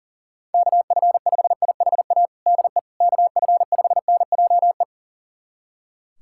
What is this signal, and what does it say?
Morse code, heard as a keyed tone, about 40 wpm, 700 Hz. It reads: KF5IHA DE KF5NJE
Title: USA Callsign DE Callsign 40wpm